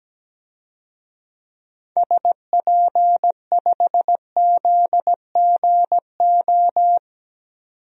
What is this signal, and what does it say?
Morse code, heard as a keyed tone, about 17 wpm, 700 Hz. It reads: SP5ZGO
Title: DXCC-100 Call Signs - 6 Characters - 17wpm